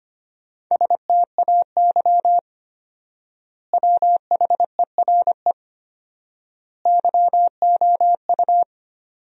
Morse code, heard as a keyed tone, about 25 wpm, 700 Hz. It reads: STAY WHERE YOU
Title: 4x Word-Spacing - Sentences from Top 500 Words - 25wpm